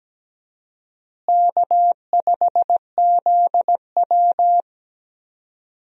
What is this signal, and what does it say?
Morse code, heard as a keyed tone, about 17 wpm, 700 Hz. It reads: K5ZW
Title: USA Call Signs - 4 Characters - Repeated 3x - 17wpm